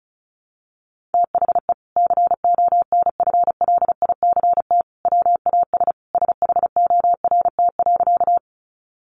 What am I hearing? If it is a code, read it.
Morse code, heard as a keyed tone, about 34 wpm, 700 Hz. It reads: THE CONFLICT WAS SHORT.
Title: Sentences from Top 1300 Words 35wpm